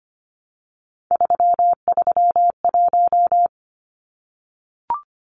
Morse code, heard as a keyed tone, about 25 wpm, 700 Hz. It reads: 331
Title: Numbers - 3-Digit 25wpm